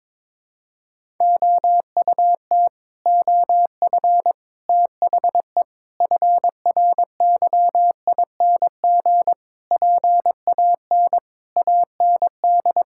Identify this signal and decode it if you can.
Morse code, heard as a keyed tone, about 22 wpm, 700 Hz. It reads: OUT OF THE FRYING PAN AND
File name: Sets of 9 Words - 22wpm